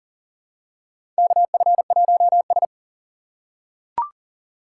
Morse code, heard as a keyed tone, about 40 wpm, 700 Hz. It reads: KF1S
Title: USA Call Signs - 2x Character Spacing - 40wpm